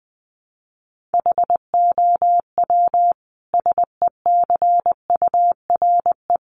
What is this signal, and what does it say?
Morse code, heard as a keyed tone, about 20 wpm, 700 Hz. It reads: HOW SECURE
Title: Sentences from Top 1200 Words 20wpm